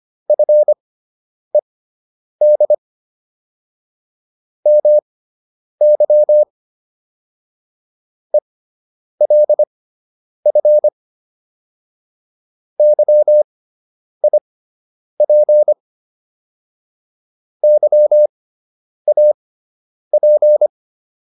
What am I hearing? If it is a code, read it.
Morse code, heard as a keyed tone, about 25 wpm, spaced as for 9 wpm, 600 Hz. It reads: FED MY ELF YIP YAP